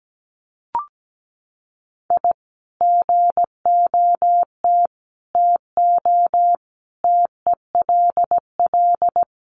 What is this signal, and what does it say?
Morse code, heard as a keyed tone, about 17 wpm, 700 Hz. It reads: I GOT TO TELL
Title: Sets of 5 Words Encore2 17wpm